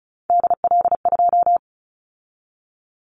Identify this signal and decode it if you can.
Morse code, heard as a keyed tone, about 36 wpm, 700 Hz. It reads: DL2
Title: Callsign Prefixes for Top 50 DXCC - 35wpm